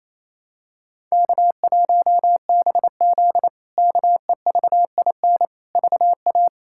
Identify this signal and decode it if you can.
Morse code, heard as a keyed tone, about 28 wpm, 700 Hz. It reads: K1BZ KEVIN VA
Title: K1USN SST QSO Practice - 28wpm